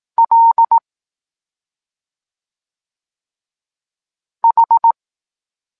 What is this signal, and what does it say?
Morse code, heard as a keyed tone, about 18 wpm, 920 Hz. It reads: L H
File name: Tekst-23 4